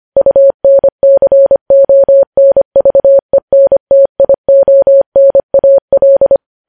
Morse code, heard as a keyed tone, about 25 wpm, 550 Hz. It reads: UNCONVENTIONAL
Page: news-headlines-in-morse-code-at-25-wpm